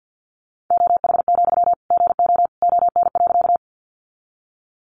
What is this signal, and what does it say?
Morse code, heard as a keyed tone, about 50 wpm, 700 Hz. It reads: OH, GO ON.
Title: Sentences from Top 500 Words 50wpm